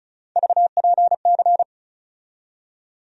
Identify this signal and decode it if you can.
Morse code, heard as a keyed tone, about 36 wpm, 700 Hz. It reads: VPC
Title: USA Call Sign Postfixes - 3 Letters 35wpm